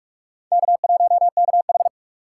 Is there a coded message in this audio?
Yes: Morse code from 0.5 to 1.9 s, heard as a keyed tone, about 41 wpm, 700 Hz. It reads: K1KH